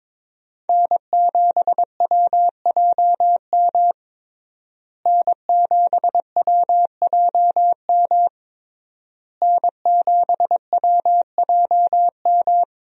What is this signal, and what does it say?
Morse code, heard as a keyed tone, about 22 wpm, 700 Hz. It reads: N7WJM N7WJM N7WJM
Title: USA Call Signs - 5 Characters - Repeated 3x - 22wpm